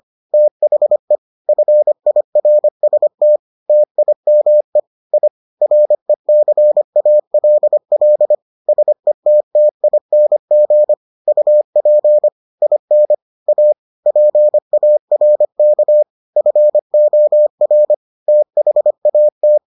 Morse code, heard as a keyed tone, about 25 wpm, 600 Hz. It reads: THE FIRST TIME I RECALL SETTING UP IN A PARK FOR THAT